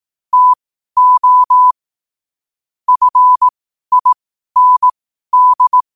Morse code, heard as a keyed tone, about 18 wpm, spaced as for 12 wpm, 1000 Hz. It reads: TO FIND
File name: qotdfiles / 12wpm